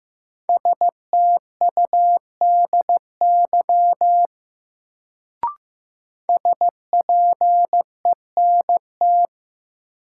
Morse code, heard as a keyed tone, about 15 wpm, 700 Hz. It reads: STUDY SPENT
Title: Sets of 5 Words - New Word Drill - 15wpm